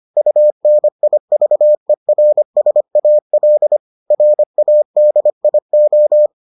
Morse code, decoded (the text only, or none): UNIVERSAL RADIO